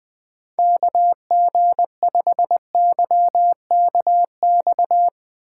Morse code, heard as a keyed tone, about 20 wpm, 700 Hz. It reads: KG5YKX